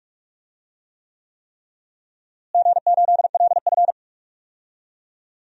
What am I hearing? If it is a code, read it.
Morse code, heard as a keyed tone, about 46 wpm, 700 Hz. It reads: G8LF
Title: DXCC-100 Call Signs - 4 Characters - Repeated 3x - 45wpm